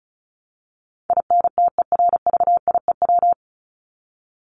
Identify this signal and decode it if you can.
Morse code, heard as a keyed tone, about 34 wpm, 700 Hz. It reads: INTERVIEW
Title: Top 1501-1750 Words 35wpm